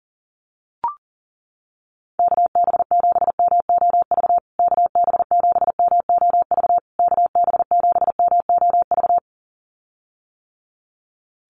Morse code, heard as a keyed tone, about 40 wpm, 700 Hz. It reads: KB7MOV KB7MOV KB7MOV
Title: USA Call Signs Repeated 3x 40wpm